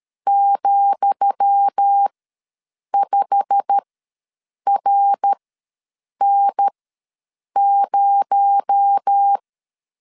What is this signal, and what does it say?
Morse code, heard as a keyed tone, about 13 wpm, spaced as for 7 wpm, 800 Hz. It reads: ,5RN0
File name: LEZIONI / 10 SIGNS / 10D COMA ws 12wpm 3618